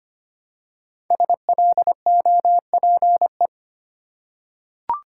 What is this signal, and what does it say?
Morse code, heard as a keyed tone, about 25 wpm, 700 Hz. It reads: SLOPE
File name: 5 Letter Words - First 3 Letters Given - 25wpm